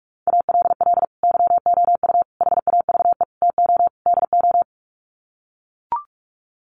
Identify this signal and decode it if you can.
Morse code, heard as a keyed tone, about 44 wpm, 700 Hz. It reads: ALL YOU HAVE TO DO
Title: Sets of 5 Words 45wpm